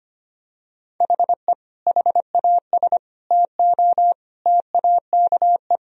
Morse code, heard as a keyed tone, about 25 wpm, 700 Hz. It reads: HE HAS TO TAKE
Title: Sentences from Top 1000 Words - Review Part 2 - 25wpm